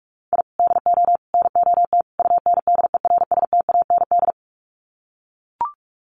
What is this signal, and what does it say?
Morse code, heard as a keyed tone, about 44 wpm, 700 Hz. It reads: I DO NOT UNDERSTAND
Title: Sets of 4 Words 45wpm